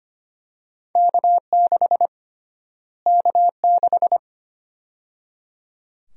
Morse code, heard as a keyed tone, about 25 wpm, 700 Hz. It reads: K6 K6
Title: USA Call Sign Prefixes - Repeated 2x - 25wpm